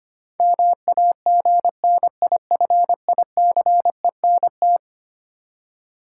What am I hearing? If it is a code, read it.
Morse code, heard as a keyed tone, about 25 wpm, 700 Hz. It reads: MAGNIFICENT
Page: morse-code-ninja-practice-25wpm